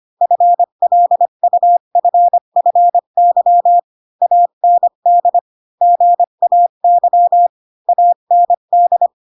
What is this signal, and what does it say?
Morse code, heard as a keyed tone, about 25 wpm, 700 Hz. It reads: FLUFFY AND GAY AND
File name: Pinocchio / Pinocchio ch5 20 25 ik1hge